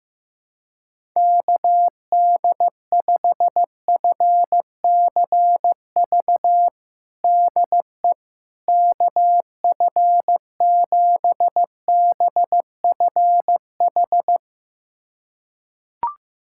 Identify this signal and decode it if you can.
Morse code, heard as a keyed tone, about 15 wpm, 700 Hz. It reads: KD5FCV DE KF7BFH
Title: USA Callsign DE Callsign 15wpm